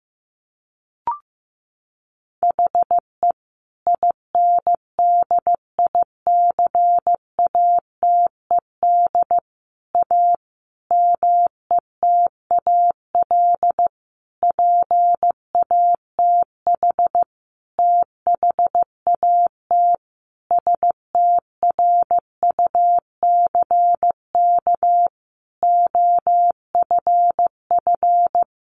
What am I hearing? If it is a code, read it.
Morse code, heard as a keyed tone, about 15 wpm, 700 Hz. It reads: HE INDICATED A METAL PATH THAT STRUCK OFF
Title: A Sound Of Thunder 15wpm